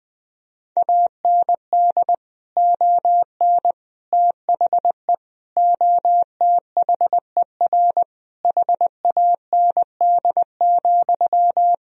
Morse code, heard as a keyed tone, about 21 wpm, 700 Hz. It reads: AND ON THE OTHER HAND,
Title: Sentences from Top 1000 Words - Review Part 3 - 20wpm